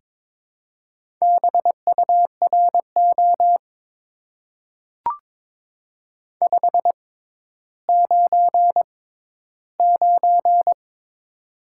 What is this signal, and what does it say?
Morse code, heard as a keyed tone, about 22 wpm, spaced as for 12 wpm, 700 Hz. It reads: BURO 599
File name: Top 90 QSO Elements - 6x Character Spacing - 22wpm